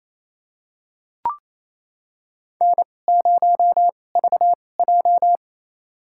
Morse code, heard as a keyed tone, about 27 wpm, spaced as for 20 wpm, 700 Hz. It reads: N0VJ
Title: USA Call Signs - 2x Character Spacing - 28wpm